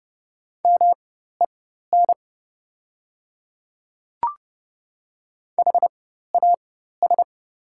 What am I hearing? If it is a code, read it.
Morse code, heard as a keyed tone, about 30 wpm, spaced as for 14 wpm, 690 Hz. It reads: MEN HAS
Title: 4x Character Spacing - Top 500 Words - 30wpm